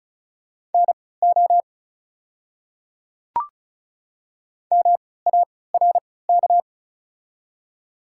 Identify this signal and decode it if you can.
Morse code, heard as a keyed tone, about 36 wpm, spaced as for 20 wpm, 690 Hz. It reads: NO MARK
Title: 3x Character Spacing - Top 300 Words - 35wpm